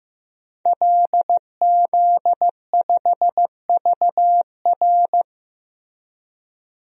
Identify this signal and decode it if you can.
Morse code, heard as a keyed tone, about 15 wpm, 700 Hz. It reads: LZ5VR
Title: DXCC-100 Call Signs - 5 Characters - Repeated 2x - 15wpm